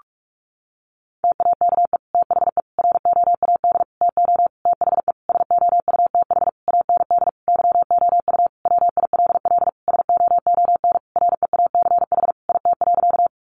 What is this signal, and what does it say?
Morse code, heard as a keyed tone, about 44 wpm, 700 Hz. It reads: TAKE THE ROAD TO THE SOUTH AND YOU WILL SOON REACH IT.
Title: Sentences from Top 400 Words 45wpm